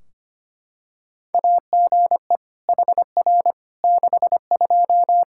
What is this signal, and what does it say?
Morse code, heard as a keyed tone, about 25 wpm, 700 Hz. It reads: AGE HR 62
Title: Practice QSO - Exchange 3 - More Info - 25wpm